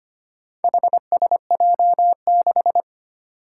Morse code, heard as a keyed tone, about 25 wpm, 700 Hz. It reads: HSJ6
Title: Four Challenging Letter-Number Combinations 25wpm